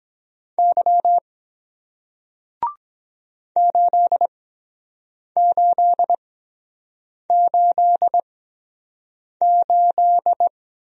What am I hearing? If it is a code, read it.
Morse code, heard as a keyed tone, about 23 wpm, spaced as for 7 wpm, 700 Hz. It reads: Y 8888